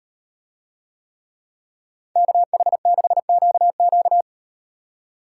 Morse code, heard as a keyed tone, about 38 wpm, 700 Hz. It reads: KH6QQ